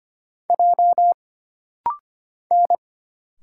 Morse code, heard as a keyed tone, about 25 wpm, 700 Hz. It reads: J N
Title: Single Letter-Number Alt - Rapid-Fire 25wpm